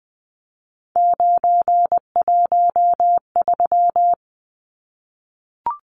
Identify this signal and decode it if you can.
Morse code, heard as a keyed tone, about 20 wpm, 700 Hz. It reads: 913
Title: Numbers - 3-Digit 20wpm